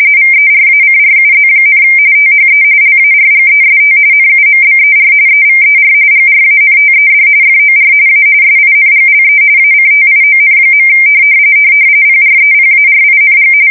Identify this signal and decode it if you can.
RTTY, about 45.45 baud. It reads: VERACRUZ, THE LARGEST NUMBER EVER GRANTED FOR A SINGLE ACTION. PREPARING
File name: fsk170 / usnavy4